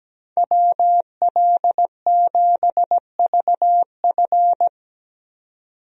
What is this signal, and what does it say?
Morse code, heard as a keyed tone, about 17 wpm, 700 Hz. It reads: WL7VF
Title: USA Call Signs - 5 Characters - 17wpm